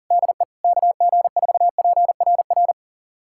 Morse code, heard as a keyed tone, about 40 wpm, 700 Hz. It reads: DE KG4PRR
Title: USA Callsign DE Callsign - 6 Characters - 40wpm